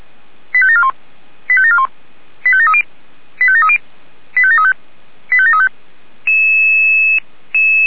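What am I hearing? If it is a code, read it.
Five-tone selective call (ZVEI), heard as quick runs of five tones, tones of about 70 ms. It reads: 85741 85741 85730 85730 85735 85735
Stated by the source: ZVEI five-tone selective call received on an SDR.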